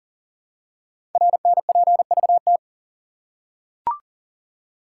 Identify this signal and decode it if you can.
Morse code, heard as a keyed tone, about 40 wpm, 700 Hz. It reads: RNPVT
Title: Code Groups - 5 Letters - 40wpm